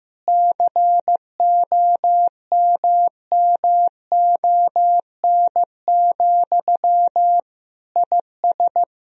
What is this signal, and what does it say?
Morse code, heard as a keyed tone, about 15 wpm, 700 Hz. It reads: COMMON, IS
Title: Sentences from Top 1000 Words - Review Part 3 - 15wpm